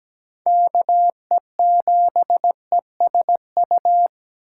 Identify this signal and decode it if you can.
Morse code, heard as a keyed tone, about 17 wpm, 700 Hz. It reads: KE7ESU
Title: USA Call Signs - 6 Characters - 3x Character Spacing - 17wpm